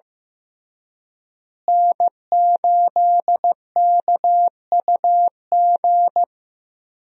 Morse code, heard as a keyed tone, about 16 wpm, 700 Hz. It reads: N8KUG
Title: USA Call Signs - 5 Characters - Repeated 3x - 15wpm